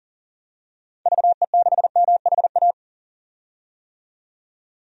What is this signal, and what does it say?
Morse code, heard as a keyed tone, about 37 wpm, 700 Hz. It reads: VE6MHA
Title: DXCC-100 Call Signs - 6 Characters - 40wpm